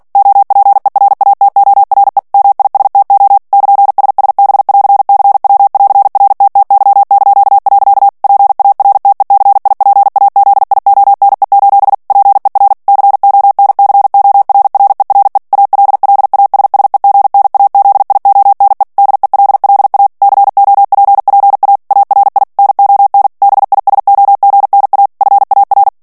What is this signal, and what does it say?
Morse code, heard as a keyed tone, about 48 wpm, 800 Hz. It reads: OPERATORE MISTO <KN>SSB<AR>CW<AR>RTTY). PARTECIPAZIONE: PER CONCORRERE ALLASSEGNAZIONE DELLA COPPA ARI NON BISOGNA FAR
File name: ITALIEN texte clair 2 vitesse 45 mots minute